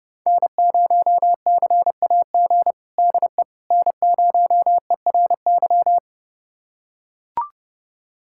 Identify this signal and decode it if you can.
Morse code, heard as a keyed tone, about 30 wpm, 700 Hz. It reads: N0CAG DE N0ERY
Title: USA Callsign DE Callsign - 5 Characters - 30wpm